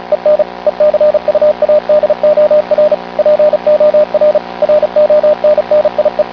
Morse code, heard as a keyed tone, about 36 wpm, 600 Hz, with radio noise. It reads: EN ECUADOR POR RONNIE